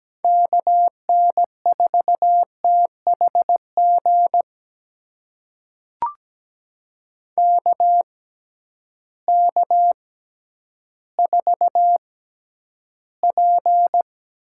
Morse code, heard as a keyed tone, about 17 wpm, spaced as for 10 wpm, 700 Hz. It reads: KN4THG KK4P
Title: USA Call Signs - 6 Characters - 6x Character Spacing - 17wpm